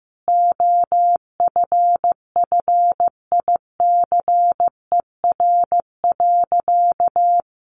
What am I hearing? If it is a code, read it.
Morse code, heard as a keyed tone, about 15 wpm, 700 Hz. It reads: OFFICER.
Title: Sentences from Top 1200 Words 15wpm